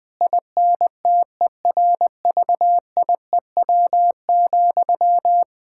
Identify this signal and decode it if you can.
Morse code, heard as a keyed tone, about 20 wpm, 700 Hz. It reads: INTERVIEW,
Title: Sentences from Top 1600 Words 20wpm